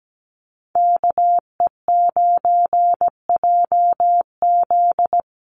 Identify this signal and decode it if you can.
Morse code, heard as a keyed tone, about 17 wpm, 700 Hz. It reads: KE9JZ
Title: USA Call Signs 17wpm